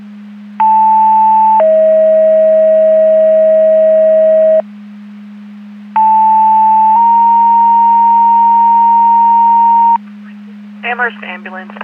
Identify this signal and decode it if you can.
Two-tone page, heard as a one-second tone, then a three-second tone, about 880 then 640 Hz, about 880 then 930 Hz, with speech. It is heard from 0.6 to 10.0 s.